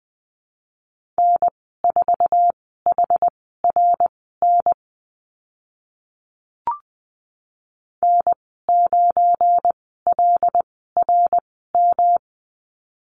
Morse code, heard as a keyed tone, about 20 wpm, spaced as for 14 wpm, 700 Hz. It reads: N4HRN N9LRM
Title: 173 Lesson 28g - Number 9 - Call Signs 20wpm